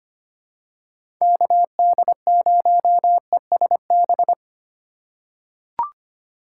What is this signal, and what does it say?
Morse code, heard as a keyed tone, about 25 wpm, 700 Hz. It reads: KD0ESB
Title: USA Call Signs - 6 Characters - 4x Character Spacing - 25wpm